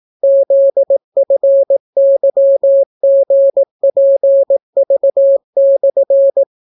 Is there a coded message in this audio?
Yes: Morse code from 0.2 to 6.4 s, heard as a keyed tone, about 18 wpm, 550 Hz. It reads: ZFYGPV/